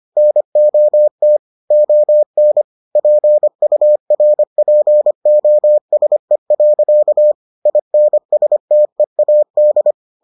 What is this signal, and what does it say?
Morse code, heard as a keyed tone, about 25 wpm, 600 Hz. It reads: NOT ON PURPOSE. INSTEAD